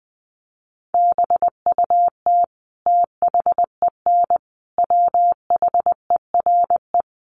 Morse code, heard as a keyed tone, about 20 wpm, 700 Hz. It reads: BUT THEN WHERE
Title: Sentences from Top 1300 Words 20wpm